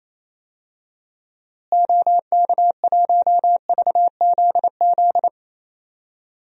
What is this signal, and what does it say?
Morse code, heard as a keyed tone, about 27 wpm, 700 Hz. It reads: OK1VZZ
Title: DXCC-100 Call Signs - 6 Characters - Repeated 2x - 28wpm